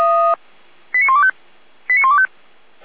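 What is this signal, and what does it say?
Five-tone selective call (ZVEI), heard as quick runs of five tones, tones of about 80 ms. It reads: 89125 89125
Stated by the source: ZVEI five-tone selective call received on a software-defined radio.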